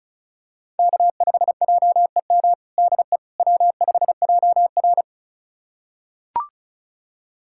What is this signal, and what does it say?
Morse code, heard as a keyed tone, about 34 wpm, 700 Hz. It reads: K5JEM DE W5JR
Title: USA Callsign DE Callsign 35wpm